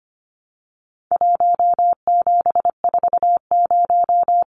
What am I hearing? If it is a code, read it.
Morse code, heard as a keyed tone, about 25 wpm, 700 Hz. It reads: 1740
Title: Numbers - 4-Digit 25wpm